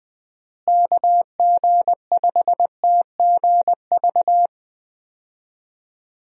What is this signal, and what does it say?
Morse code, heard as a keyed tone, about 20 wpm, 700 Hz. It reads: KG5TGV
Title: USA Call Signs - 6 Characters - Repeated 3x - 20wpm